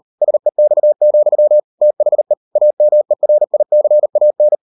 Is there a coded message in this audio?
Yes: Morse code from 0.2 to 4.6 s, heard as a keyed tone, about 38 wpm, 600 Hz. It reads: SEX, THE AMERICAN